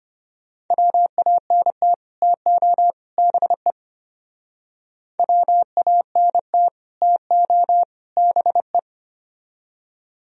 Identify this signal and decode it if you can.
Morse code, heard as a keyed tone, about 31 wpm, 700 Hz. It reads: WANT TO BE WANT TO BE
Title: Speed-Racing - Sets of 3 Words 25wpm